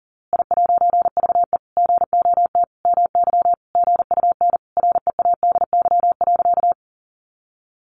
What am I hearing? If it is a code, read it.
Morse code, heard as a keyed tone, about 40 wpm, 700 Hz. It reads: I'VE GOT MY GUN READY.
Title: A Sound Of Thunder 40wpm